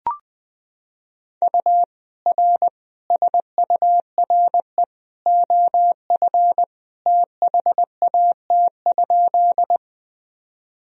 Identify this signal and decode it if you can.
Morse code, heard as a keyed tone, about 20 wpm, 700 Hz. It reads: U R SURE OF THAT?
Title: Sentences from Top 300 Words and Abbreviations 20wpm